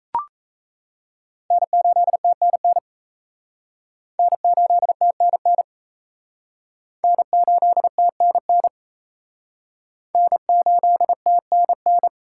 Morse code, heard as a keyed tone, about 37 wpm, 700 Hz. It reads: N8TNN N8TNN N8TNN N8TNN